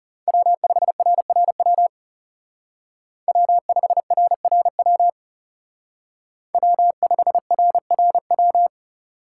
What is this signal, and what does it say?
Morse code, heard as a keyed tone, about 36 wpm, 700 Hz. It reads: W5RRW W5RRW W5RRW
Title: Speed Racing - USA Call Signs 30wpm